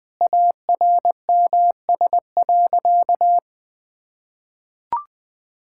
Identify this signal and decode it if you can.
Morse code, heard as a keyed tone, about 19 wpm, 700 Hz. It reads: ARMS.
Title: Sentences with 8-Word Phrases - 20wpm